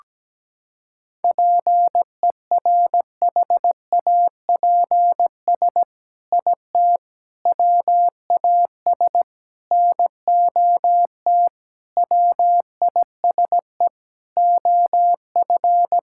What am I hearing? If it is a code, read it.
Morse code, heard as a keyed tone, about 17 wpm, 700 Hz. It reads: PERHAPS IT WAS NOT WISE OF
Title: Sentences from Top 1600 Words 17wpm